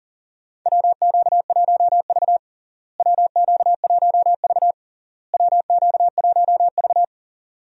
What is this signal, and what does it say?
Morse code, heard as a keyed tone, about 40 wpm, 700 Hz. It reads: WQ1V WQ1V WQ1V
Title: USA Call Signs - 4 Characters - Repeated 3x - 40wpm